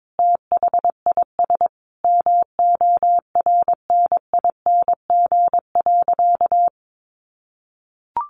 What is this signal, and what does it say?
Morse code, heard as a keyed tone, about 22 wpm, 700 Hz. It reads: THIS MORNING.
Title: Sentences from Top 1100 Words 22wpm